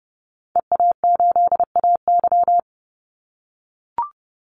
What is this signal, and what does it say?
Morse code, heard as a keyed tone, about 30 wpm, 700 Hz. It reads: EA8AY
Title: Call Signs - 30wpm